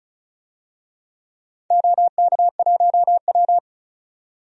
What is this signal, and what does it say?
Morse code, heard as a keyed tone, about 36 wpm, 700 Hz. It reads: OK1W
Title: DXCC-100 Call Signs - 4 Characters - 35wpm